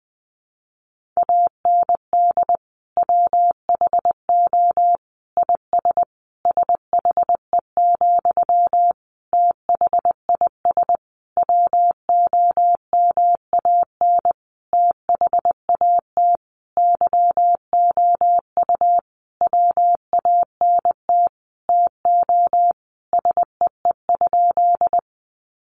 Morse code, heard as a keyed tone, about 20 wpm, 700 Hz. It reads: AND WHO IS SHE, THIS WOMAN THAT YOU WANT TO SEE?
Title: Sentences from Top 1100 Words 20wpm